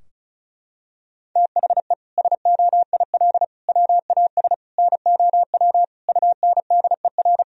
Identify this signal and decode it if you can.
Morse code, heard as a keyed tone, about 34 wpm, 700 Hz. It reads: THE SOIL WAS NOW UNDER
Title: Sentences from Top 1000 Words - Review Part 4 - 35wpm